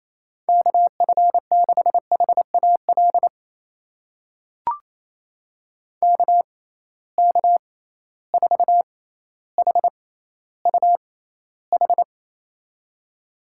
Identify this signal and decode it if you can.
Morse code, heard as a keyed tone, about 28 wpm, spaced as for 14 wpm, 700 Hz. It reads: KF6HAL KK4HUH
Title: USA Call Signs - 6x Character Spacing - 28wpm